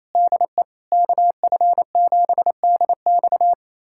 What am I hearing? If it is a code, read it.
Morse code, heard as a keyed tone, about 28 wpm, 700 Hz. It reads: DE KF7DX
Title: USA Callsign DE Callsign - 5 Characters - 28wpm